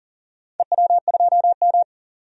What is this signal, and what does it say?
Morse code, heard as a keyed tone, about 40 wpm, 700 Hz. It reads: EW2M
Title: DXCC-100 Call Signs - 4 Characters - 40wpm